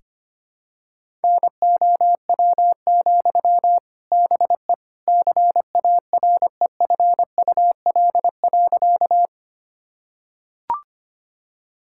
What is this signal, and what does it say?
Morse code, heard as a keyed tone, about 25 wpm, 700 Hz. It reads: NOW, BE CAREFUL.
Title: Sentences from Top 1600 Words 25wpm